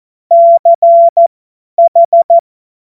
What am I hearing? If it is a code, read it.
Morse code, heard as a keyed tone, about 14 wpm, spaced as for 10 wpm, 680 Hz. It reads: CH
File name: Morsekurs / Witze / Witz 23. Dezember 2025 50 F70